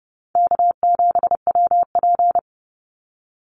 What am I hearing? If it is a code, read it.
Morse code, heard as a keyed tone, about 30 wpm, 700 Hz. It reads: K7WP
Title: Easy Call Signs Repeated 2x - 30wpm